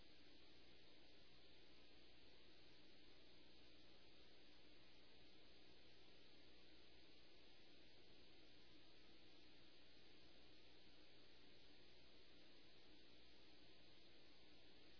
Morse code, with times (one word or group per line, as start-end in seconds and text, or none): none